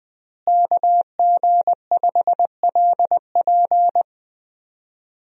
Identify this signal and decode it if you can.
Morse code, heard as a keyed tone, about 20 wpm, 700 Hz. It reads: KG5LP
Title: USA Call Signs - 5 Characters - Repeated 3x - 20wpm